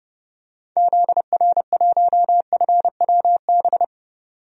Morse code, heard as a keyed tone, about 29 wpm, 700 Hz. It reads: ZR1FWB
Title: DXCC-100 Call Signs - 6 Characters - 30wpm